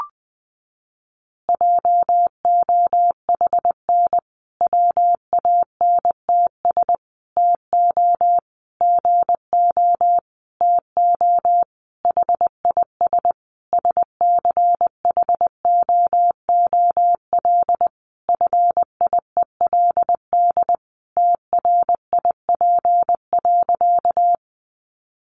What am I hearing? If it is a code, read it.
Morse code, heard as a keyed tone, about 19 wpm, 700 Hz. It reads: JOHN WANTS TO GO TO HIS SCHOOL FIELD TRIP.
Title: Additional Easy Phrases 20wpm